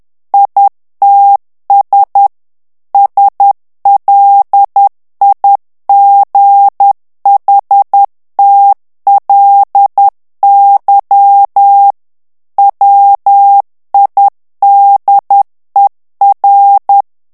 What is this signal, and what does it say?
Morse code, heard as a keyed tone, about 10 wpm, 800 Hz. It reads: ITS SLIGHTLY WIDER
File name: ANGLAIS texte clair 3 vitesse 10 mots minute